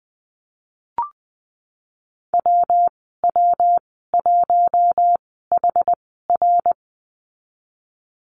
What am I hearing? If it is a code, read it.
Morse code, heard as a keyed tone, about 20 wpm, spaced as for 15 wpm, 700 Hz. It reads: WW1HR
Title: 173 Lesson 28g - Number 9 - Call Signs 20wpm